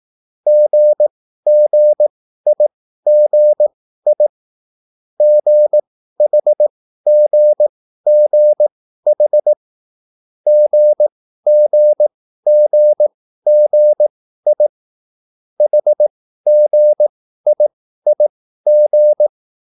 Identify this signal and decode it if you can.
Morse code, heard as a keyed tone, about 18 wpm, spaced as for 13 wpm, 600 Hz. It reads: GGIGI GHGGH GGGGI HGIIG
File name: QSL / CW lesson